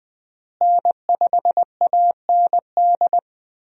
Morse code, heard as a keyed tone, about 20 wpm, 700 Hz. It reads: N5AND
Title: USA Call Signs - 5 Characters - Repeated 2x - 20wpm